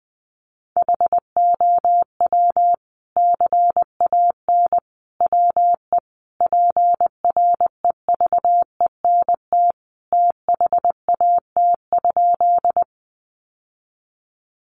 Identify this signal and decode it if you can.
Morse code, heard as a keyed tone, about 20 wpm, 700 Hz. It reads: HOW CAN WE PREVENT THAT?
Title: Sentences from Top 1100 Words 20wpm